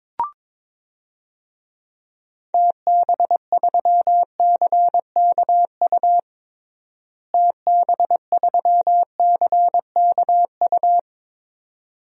Morse code, heard as a keyed tone, about 22 wpm, 700 Hz. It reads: TB3CKU TB3CKU
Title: DXCC-100 Call Signs - 6 Characters - Repeated 2x - 22wpm